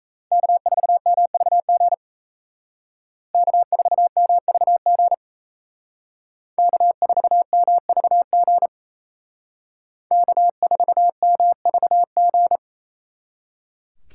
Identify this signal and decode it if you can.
Morse code, heard as a keyed tone, about 35 wpm, 700 Hz. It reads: K4MVG K4MVG K4MVG K4MVG
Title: Speed Racing - USA Call Signs 28wpm